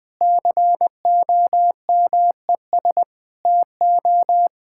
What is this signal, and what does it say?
Morse code, heard as a keyed tone, about 19 wpm, 700 Hz. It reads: COMES TO
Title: Sets of 8 Words - 20wpm